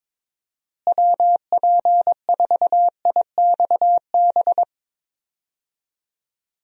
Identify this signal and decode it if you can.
Morse code, heard as a keyed tone, about 22 wpm, 700 Hz. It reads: WP4IXB
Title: USA Call Signs - 6 Characters - Repeated 2x - 22wpm